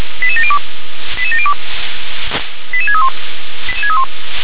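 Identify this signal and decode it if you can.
Five-tone selective call (ZVEI), heard as quick runs of five tones, tones of about 80 ms. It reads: 88702 88702 88631
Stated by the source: ZVEI five-tone selective call 88911 received by SDR.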